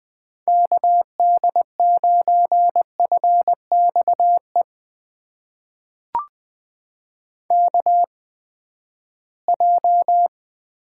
Morse code, heard as a keyed tone, about 20 wpm, spaced as for 14 wpm, 700 Hz. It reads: KD9FXE KJ
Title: USA Call Signs - 6 Characters - 8x Character Spacing - 20wpm